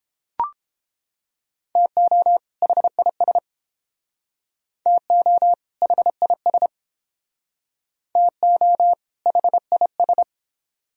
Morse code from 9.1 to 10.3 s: HIS

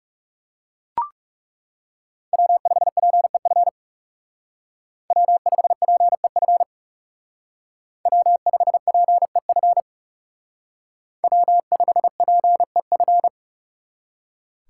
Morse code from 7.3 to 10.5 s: W5PEF

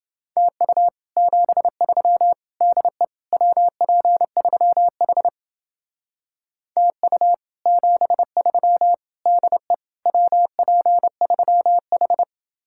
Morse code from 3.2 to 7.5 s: WP3H TU

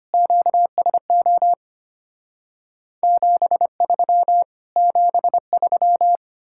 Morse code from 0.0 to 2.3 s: QSO

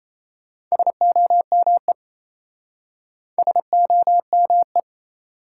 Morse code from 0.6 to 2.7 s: SOME

